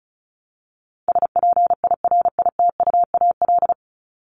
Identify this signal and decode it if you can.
Morse code, heard as a keyed tone, about 34 wpm, 700 Hz. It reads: SPIRITUAL
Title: Top 1501-1750 Words 35wpm